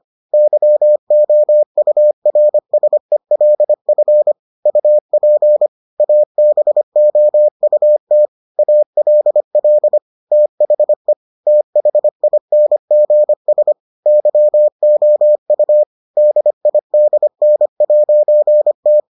Morse code, heard as a keyed tone, about 25 wpm, 600 Hz. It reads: YOURSELF UP ABOUT ALL THE THINGS YOU DIDN'T